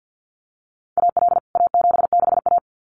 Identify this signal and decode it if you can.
Morse code, heard as a keyed tone, about 49 wpm, 700 Hz. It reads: AL A76A
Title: CWT Exchange 50wpm